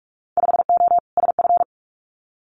Morse code, heard as a keyed tone, about 46 wpm, 700 Hz. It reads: 5O SF
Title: Winter Field Day Exchanges Encore 45wpm